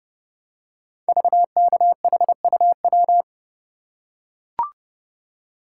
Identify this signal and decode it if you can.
Morse code, heard as a keyed tone, about 30 wpm, 690 Hz. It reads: VKHUW
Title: Code Groups - 5 Letters - 30wpm